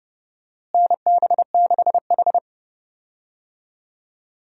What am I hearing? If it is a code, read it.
Morse code, heard as a keyed tone, about 30 wpm, 700 Hz. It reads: NB6H
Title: USA Call Signs - 4 Characters - Repeated 3x - 30wpm